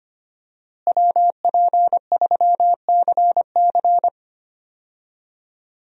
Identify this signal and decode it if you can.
Morse code, heard as a keyed tone, about 25 wpm, 700 Hz. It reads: WP3CC
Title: USA Call Signs - 5 Characters - Repeated 2x - 25wpm